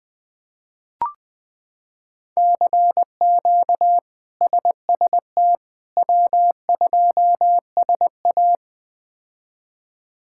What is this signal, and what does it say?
Morse code, heard as a keyed tone, about 20 wpm, 700 Hz. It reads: CQ SST W2SA
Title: K1USN SST QSO Practice - 20wpm